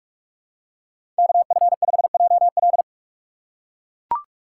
Morse code, heard as a keyed tone, about 44 wpm, 690 Hz. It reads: KF5JL
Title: USA Call Signs - 5 Characters - 2x Character Spacing - 45wpm